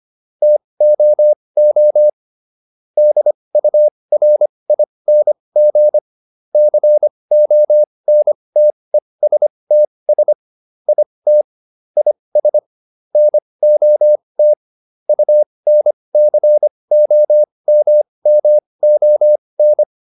Morse code, heard as a keyed tone, about 25 wpm, spaced as for 20 wpm, 600 Hz. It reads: TOO DURING CONTESTS IT IS NOT UNCOMMON